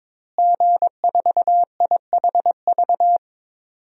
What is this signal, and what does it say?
Morse code, heard as a keyed tone, about 21 wpm, 700 Hz. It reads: G4IHV
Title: DXCC-100 Call Signs - 5 Characters - Repeated 2x - 22wpm